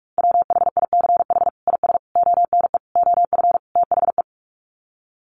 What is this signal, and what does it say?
Morse code, heard as a keyed tone, about 44 wpm, 700 Hz. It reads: WHICH IS ONE OF THE
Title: Sets of 5 Words Encore 45wpm